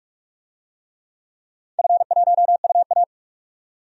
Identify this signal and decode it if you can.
Morse code, heard as a keyed tone, about 46 wpm, 700 Hz. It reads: F1UA
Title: DXCC-100 Call Signs - 4 Characters - 45wpm